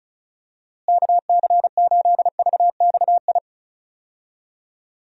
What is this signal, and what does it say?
Morse code, heard as a keyed tone, about 34 wpm, 700 Hz. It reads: KC8VXI
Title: USA Call Signs - 6 Characters - Repeated 3x - 35wpm